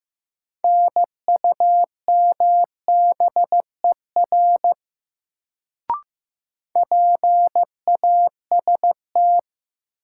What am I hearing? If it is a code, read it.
Morse code, heard as a keyed tone, about 16 wpm, 700 Hz. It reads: NUMBER PAST
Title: Sets of 2 Words - New Word Drill - 15wpm